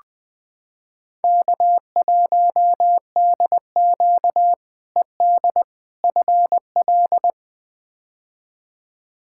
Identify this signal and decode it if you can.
Morse code, heard as a keyed tone, about 20 wpm, 700 Hz. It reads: K1DQ ED FL
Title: K1USN SST QSO Practice - 20wpm